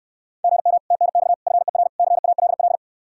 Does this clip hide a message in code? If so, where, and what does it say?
Yes: Morse code from 0.4 to 2.8 s, heard as a keyed tone, about 68 wpm, 700 Hz. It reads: GM TNX FER CALL